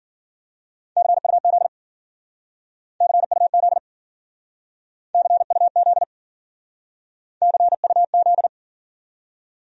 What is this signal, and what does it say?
Morse code, heard as a keyed tone, about 52 wpm, 700 Hz. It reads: CUZ CUZ CUZ CUZ